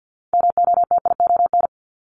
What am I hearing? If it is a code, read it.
Morse code, heard as a keyed tone, about 50 wpm, 700 Hz. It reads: MOTION
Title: Multi-letter Phonemes 50wpm